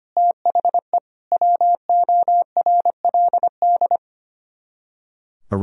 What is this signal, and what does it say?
Morse code, heard as a keyed tone, about 31 wpm, 700 Hz. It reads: THE WORLD
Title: Speed-Racing - Sets of 3 Words 25wpm